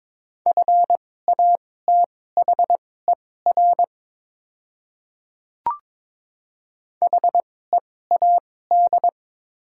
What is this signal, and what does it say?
Morse code, heard as a keyed tone, about 22 wpm, spaced as for 16 wpm, 690 Hz. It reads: FATHER HEAD
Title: 2x Character Spacing - Top 500 Words - 22wpm